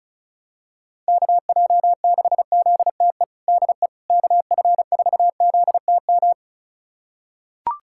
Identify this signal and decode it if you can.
Morse code, heard as a keyed tone, about 36 wpm, 700 Hz. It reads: KJ6ZTE DE KF4ZTM
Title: USA Callsign DE Callsign - 6 Characters - 35wpm